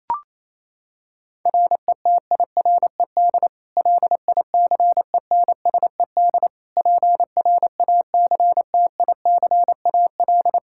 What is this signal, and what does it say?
Morse code, heard as a keyed tone, about 28 wpm, 700 Hz. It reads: RETIRED LICENSED PRACTICAL
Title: K1USN SST QSO Practice 28wpm